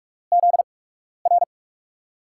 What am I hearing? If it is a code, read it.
Morse code, heard as a keyed tone, about 42 wpm, spaced as for 12 wpm, 700 Hz. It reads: ZR
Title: USA Call Signs - 5 Characters - 8x Character Spacing - 45wpm